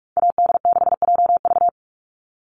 Morse code, heard as a keyed tone, about 44 wpm, 700 Hz. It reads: AD6JV
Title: Call Signs Repeated 3x - 45wpm